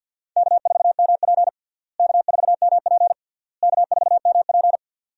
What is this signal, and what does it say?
Morse code, heard as a keyed tone, about 52 wpm, 700 Hz. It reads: K4MP K4MP K4MP
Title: USA Call Signs - 4 Characters - Repeated 3x - 50wpm